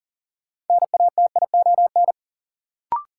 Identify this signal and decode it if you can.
Morse code, heard as a keyed tone, about 39 wpm, 700 Hz. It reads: NATION
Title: Sets of 3 Words - New Word Drill - 40wpm